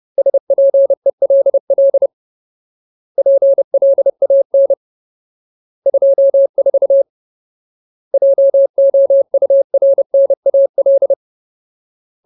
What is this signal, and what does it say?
Morse code, heard as a keyed tone, about 30 wpm, 550 Hz. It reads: SPELL PLAN 24 JOURNAL
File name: Ultimate Word List / 30WPM Ultimate